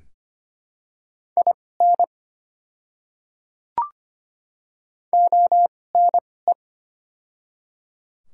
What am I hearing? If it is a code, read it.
Morse code, heard as a keyed tone, about 25 wpm, spaced as for 18 wpm, 700 Hz. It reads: IN ONE
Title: 2x Character Spacing - Top 200 Words - 25wpm